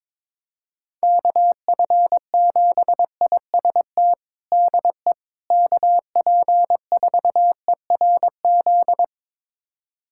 Morse code, heard as a keyed tone, about 22 wpm, 700 Hz. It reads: KF7IST DE KP4ERZ
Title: USA Callsign DE Callsign - 6 Characters - 22wpm